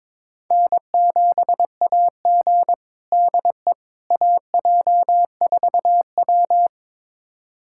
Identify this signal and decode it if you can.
Morse code, heard as a keyed tone, about 22 wpm, 700 Hz. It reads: N7AG DE AJ4W
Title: USA Callsign DE Callsign - 4 Characters - 22wpm